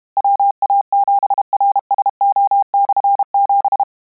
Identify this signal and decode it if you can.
Morse code, heard as a keyed tone, about 32 wpm, 800 Hz. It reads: WA7RSO/7
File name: Combinations / Combinations-mp3s / Combinations@30